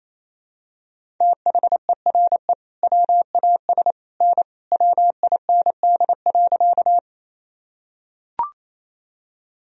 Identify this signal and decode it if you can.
Morse code, heard as a keyed tone, about 28 wpm, 700 Hz. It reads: THERE WAS N WIND.
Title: Sentences from Top 400 Words and Abbreviations 28wpm